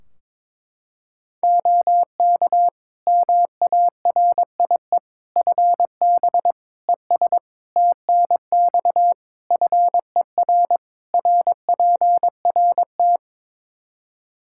Morse code, heard as a keyed tone, about 22 wpm, 700 Hz. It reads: OK MARIE FB ES TNX FER RPRT
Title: Practice QSO - Exchange 2 - Info - 22wpm